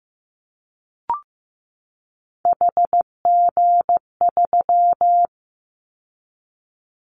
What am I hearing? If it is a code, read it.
Morse code, heard as a keyed tone, about 15 wpm, 700 Hz. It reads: HG3
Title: Callsign Prefixes for Top 50 DXCC - 15wpm